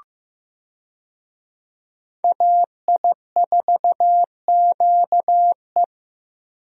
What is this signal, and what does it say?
Morse code, heard as a keyed tone, about 15 wpm, 700 Hz. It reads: AI4QE
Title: USA Call Signs - 5 Characters - Repeated 3x - 15wpm